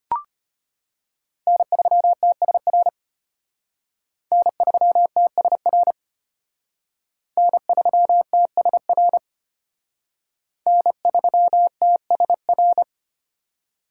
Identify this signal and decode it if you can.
Morse code, heard as a keyed tone, about 32 wpm, 700 Hz. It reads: N3TSR N3TSR N3TSR N3TSR